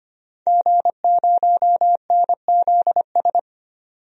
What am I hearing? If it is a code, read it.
Morse code, heard as a keyed tone, about 25 wpm, 700 Hz. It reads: G0NZS
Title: DXCC-100 Call Signs - 5 Characters - 25wpm